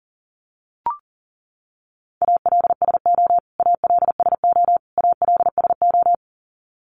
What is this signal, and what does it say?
Morse code, heard as a keyed tone, about 40 wpm, 700 Hz. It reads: ALSO ALSO ALSO
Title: Top 300 Words Repeated 40wpm